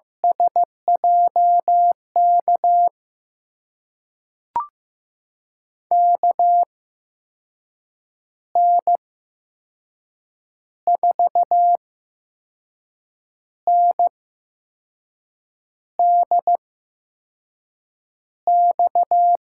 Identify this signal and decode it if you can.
Morse code, heard as a keyed tone, about 15 wpm, spaced as for 5 wpm, 700 Hz. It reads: SJK KN4NDX
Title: USA Call Signs - 6 Characters - 8x Character Spacing - 15wpm